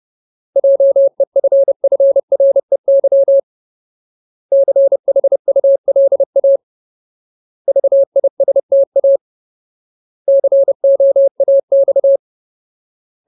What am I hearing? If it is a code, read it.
Morse code, heard as a keyed tone, about 30 wpm, 550 Hz. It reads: JEFFREY CHULA VISTA COAX
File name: Ultimate Word List / 30WPM Ultimate